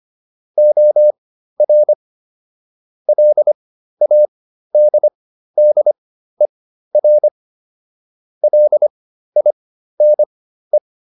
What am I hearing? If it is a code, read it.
Morse code, heard as a keyed tone, about 25 wpm, spaced as for 13 wpm, 600 Hz. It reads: OR LADDER LINE